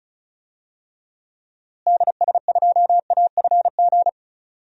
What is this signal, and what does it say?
Morse code, heard as a keyed tone, about 34 wpm, 700 Hz. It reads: DS2AFG
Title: DXCC-100 Call Signs - 6 Characters - Repeated 2x - 35wpm